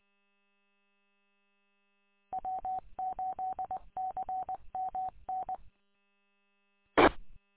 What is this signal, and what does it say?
Morse code, heard as a keyed tone, about 20 wpm, 780 Hz. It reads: W8CMN T
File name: w8cmn- -combined